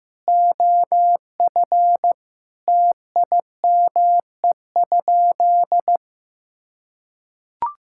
Morse code, heard as a keyed tone, about 15 wpm, 700 Hz. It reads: OF TIME?
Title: Sentences from Top 1000 Words - Review Part 5 - 15wpm